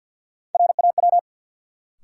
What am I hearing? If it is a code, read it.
Morse code, heard as a keyed tone, about 52 wpm, 700 Hz. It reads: RAW
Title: Sentences from Top 1500 Words - New Word Drill Alt - 50wpm